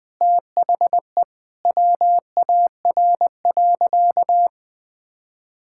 Morse code, heard as a keyed tone, about 20 wpm, 700 Hz. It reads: THE WAR.
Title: Sentences with 4-Word Phrases - Spoken Beginning - 20wpm